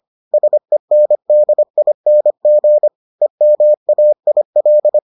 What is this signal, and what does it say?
Morse code, heard as a keyed tone, about 25 wpm, 600 Hz. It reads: SENDING EMAIL